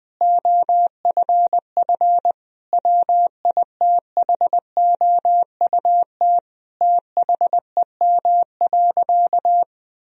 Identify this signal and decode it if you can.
Morse code, heard as a keyed tone, about 20 wpm, 700 Hz. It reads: OFF WITHOUT THEM.
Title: Sentences from Top 1000 Words - Review Part 3 - 20wpm